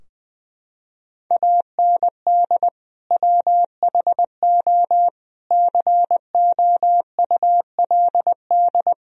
Morse code, heard as a keyed tone, about 20 wpm, 700 Hz. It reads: AND WHO COULD
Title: Sentences from Top 1000 Words - Review Part 3 - 20wpm